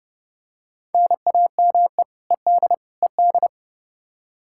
Morse code, heard as a keyed tone, about 32 wpm, 700 Hz. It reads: NAME ED ED
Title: Practice QSO - Exchange 1 - Beginning - 30wpm